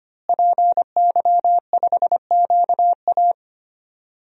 Morse code, heard as a keyed tone, about 25 wpm, 700 Hz. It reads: PY5QA
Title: DXCC-100 Call Signs - 5 Characters - Repeated 3x - 25wpm